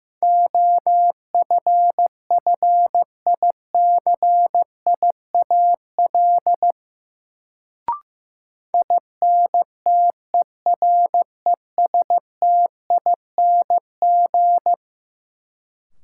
Morse code, heard as a keyed tone, about 15 wpm, 700 Hz. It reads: OFFICIAL INTERESTING
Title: Sentences from Top 1000 Words - New Word Drill - 15wpm